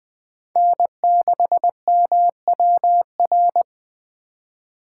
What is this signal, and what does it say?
Morse code, heard as a keyed tone, about 20 wpm, 700 Hz. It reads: N6MWR
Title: USA Call Signs - 5 Characters - 4x Character Spacing - 20wpm